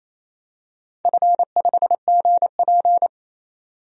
Morse code, heard as a keyed tone, about 28 wpm, 700 Hz. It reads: F5GP
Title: DXCC-100 Call Signs - 4 Characters - 28wpm